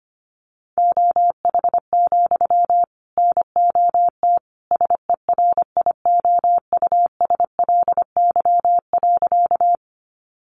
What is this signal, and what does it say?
Morse code, heard as a keyed tone, about 25 wpm, 700 Hz. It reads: OH, NOT SERIOUSLY.
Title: Sentences from Top 1200 Words 25wpm